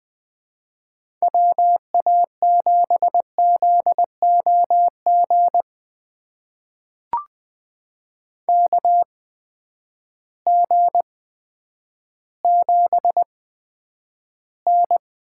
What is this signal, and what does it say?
Morse code, heard as a keyed tone, about 20 wpm, spaced as for 10 wpm, 700 Hz. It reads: WA7ZOG KG7N
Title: USA Call Signs - 6 Characters - 8x Character Spacing - 20wpm